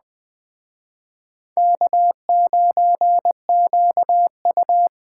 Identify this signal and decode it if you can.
Morse code, heard as a keyed tone, about 21 wpm, 700 Hz. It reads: K9QU
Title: Speed Racing - USA 1x2 Call Signs 17wpm